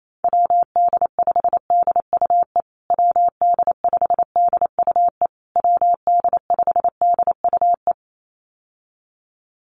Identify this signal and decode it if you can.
Morse code, heard as a keyed tone, about 28 wpm, 700 Hz. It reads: WD5DUE WD5DUE WD5DUE
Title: USA Call Signs Repeated 3x 28wpm